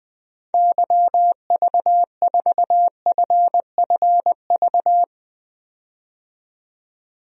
Morse code, heard as a keyed tone, about 20 wpm, 700 Hz. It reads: YV4FFV
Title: DXCC-100 Call Signs - 6 Characters - 20wpm